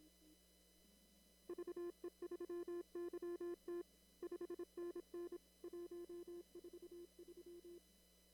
Morse code, heard as a keyed tone, about 27 wpm, 340 Hz. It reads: VE3YT 5NN 143
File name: CW / kt5j wpx cw-op-k5pi